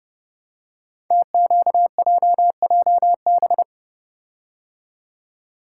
Morse code, heard as a keyed tone, about 30 wpm, 700 Hz. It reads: TQJJB
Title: Code Groups - 5 Letter:Numbers - 30wpm